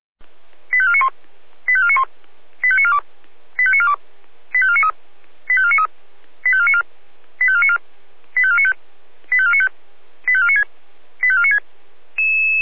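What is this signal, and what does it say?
Five-tone selective call (ZVEI), heard as quick runs of five tones, tones of about 70 ms. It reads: 85491 85491 85942 85942 85493 85493 85494 85494 85495 85495 85496 85496
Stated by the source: ZVEI five-tone selective call received by SDR, hosted by the Feuerwehr Orlamünde fire brigade.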